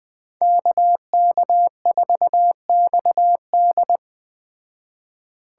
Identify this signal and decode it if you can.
Morse code, heard as a keyed tone, about 20 wpm, 700 Hz. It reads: KK4XD